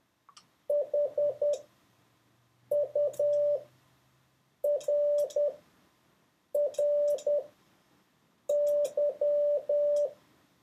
Morse code, heard as a keyed tone, about 10 wpm, spaced as for 6 wpm, 580 Hz. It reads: HURRY
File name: 4of6-words-1